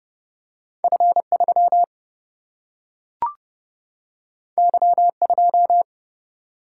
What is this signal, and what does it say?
Morse code, heard as a keyed tone, about 29 wpm, 700 Hz. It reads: F 3 Y 2
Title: Single-Letter Single-Number 30wpm